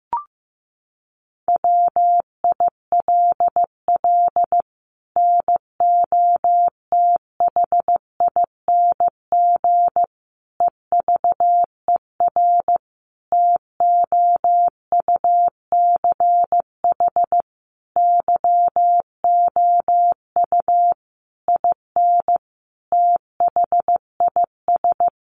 Morse code, heard as a keyed tone, about 15 wpm, 700 Hz. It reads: WILL NOTHING EVER TOUCH YOU IN THIS
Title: Sentences from Top 1200 Words 15wpm